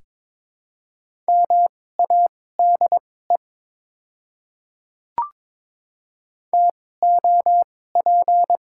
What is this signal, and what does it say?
Morse code, heard as a keyed tone, about 22 wpm, spaced as for 16 wpm, 700 Hz. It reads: MADE TOP
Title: 2x Character Spacing - Top 500 Words - 22wpm